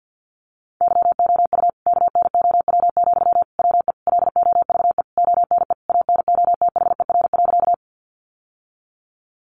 Morse code, heard as a keyed tone, about 50 wpm, 700 Hz. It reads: YOU KNOW, WE LOVE ONE ANOTHER.
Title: Sentences from Top 600 Words 50wpm